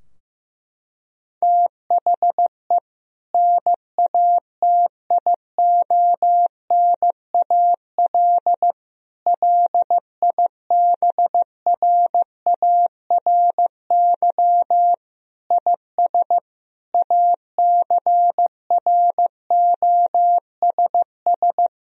Morse code, heard as a keyed tone, about 15 wpm, 700 Hz. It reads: THE NATIONAL LIBRARY IS ACROSS
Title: Sentences from Top 1000 Words - Review Part 4 - 15wpm